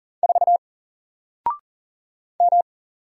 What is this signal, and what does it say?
Morse code, heard as a keyed tone, about 38 wpm, 700 Hz. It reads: 4 M
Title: Single Character (Letter:Number:Punctuation:Prosign) 40wpm